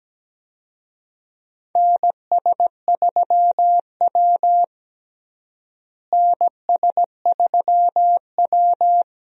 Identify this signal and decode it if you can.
Morse code, heard as a keyed tone, about 17 wpm, 700 Hz. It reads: NS3W NS3W
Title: USA Call Signs - 4 Characters - Repeated 3x - 17wpm